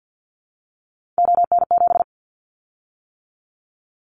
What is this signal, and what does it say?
Morse code, heard as a keyed tone, about 50 wpm, 700 Hz. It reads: ON7
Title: Callsign Prefixes for Top 50 DXCC - 50wpm